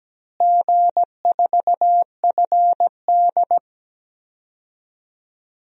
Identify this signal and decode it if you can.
Morse code, heard as a keyed tone, about 17 wpm, 700 Hz. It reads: G4FD
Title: DXCC-100 Call Signs - 4 Characters - Repeated 3x - 17wpm